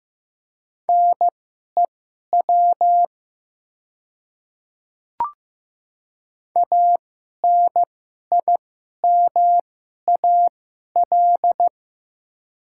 Morse code, heard as a keyed tone, about 15 wpm, spaced as for 11 wpm, 700 Hz. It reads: NEW ANIMAL
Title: 2x Character Spacing - Top 200 Words - 15wpm